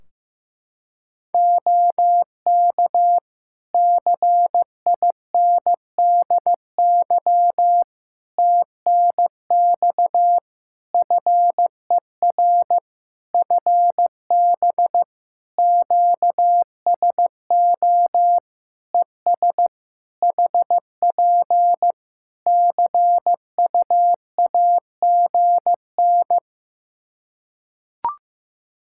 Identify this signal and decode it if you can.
Morse code, heard as a keyed tone, about 15 wpm, 700 Hz. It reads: OK CINDY TNX FER FB QSO ES HP CUAGN
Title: K1USN SST QSO Practice 15wpm